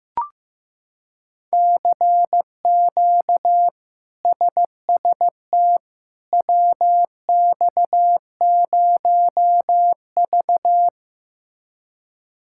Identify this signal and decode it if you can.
Morse code, heard as a keyed tone, about 15 wpm, 700 Hz. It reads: CQ SST WX0V
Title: K1USN SST QSO Practice - 15wpm